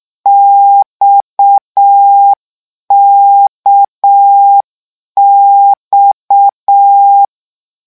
Morse code, heard as a keyed tone, about 6 wpm, 800 Hz. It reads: XKX
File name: lecon 1-vitesse 6